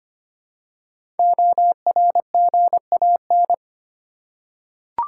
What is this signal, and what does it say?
Morse code, heard as a keyed tone, about 25 wpm, 700 Hz. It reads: ORGAN
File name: 5 Letter Words - First 3 Letters Given - 25wpm